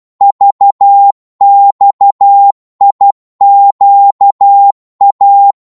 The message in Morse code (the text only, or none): VXIQA